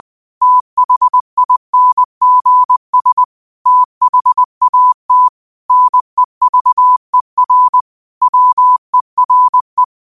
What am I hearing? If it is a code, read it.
Morse code, heard as a keyed tone, about 20 wpm, 1000 Hz. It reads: THINGS THAT NEVER WERE
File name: qotdfiles / 20wpm